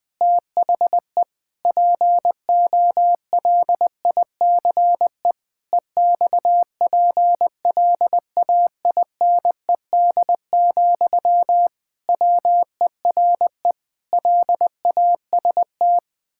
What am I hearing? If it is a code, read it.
Morse code, heard as a keyed tone, about 20 wpm, 700 Hz. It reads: THE POLICE EXPLAINED, WERE LAST
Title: Pangram Challenge v2 - 20wpm